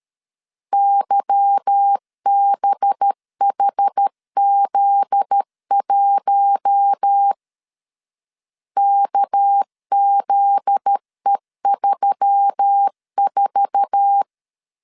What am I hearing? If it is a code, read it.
Morse code, heard as a keyed tone, about 13 wpm, 800 Hz. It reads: YBHZ1 KZE34
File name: LEZIONI / 8 YJ45 / 8 YJ4 12wpm 2151